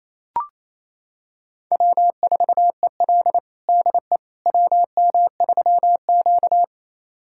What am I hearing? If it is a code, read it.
Morse code, heard as a keyed tone, about 29 wpm, 700 Hz. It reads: W4EL DE WM3Q
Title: USA Callsign DE Callsign - 4 Characters - 28wpm